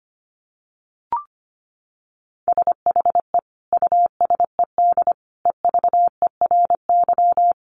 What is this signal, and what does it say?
Morse code, heard as a keyed tone, about 25 wpm, 700 Hz. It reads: SHE USED EVERY
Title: Sentences from Top 1400 Words 25wpm